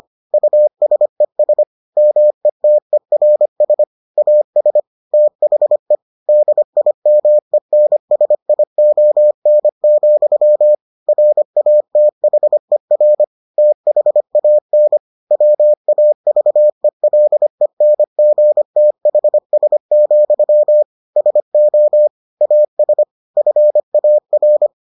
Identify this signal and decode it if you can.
Morse code, heard as a keyed tone, about 25 wpm, 600 Hz. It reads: USES METERS AS THE DIMENSION, RATHER THAN WAVELENGTHS, SO AS FAR